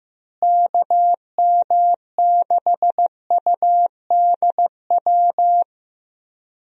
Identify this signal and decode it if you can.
Morse code, heard as a keyed tone, about 15 wpm, 700 Hz. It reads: KM6UDW
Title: USA Call Signs - 6 Characters - Repeated 3x - 15wpm